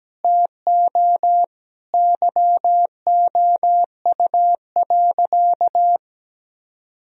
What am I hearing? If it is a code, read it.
Morse code, heard as a keyed tone, about 17 wpm, 700 Hz. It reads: TO YOU.
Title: Sentences from Top 1000 Words - Review Part 1 - 17wpm